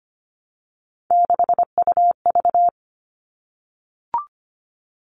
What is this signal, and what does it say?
Morse code, heard as a keyed tone, about 25 wpm, 700 Hz. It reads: 6UV
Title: All Cut Numbers - 3-Digit 25wpm